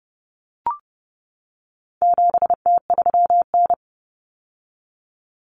Morse code, heard as a keyed tone, about 30 wpm, 700 Hz. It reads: 7T3N
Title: Common Cut Numbers - 4-Digit 30wpm